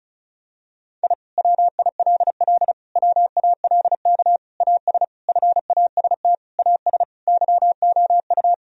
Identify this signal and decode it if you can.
Morse code, heard as a keyed tone, about 36 wpm, 700 Hz. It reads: I WILL WALK AS FAST AS YOU
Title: Sentences from Top 1000 Words - Review Part 2 - 35wpm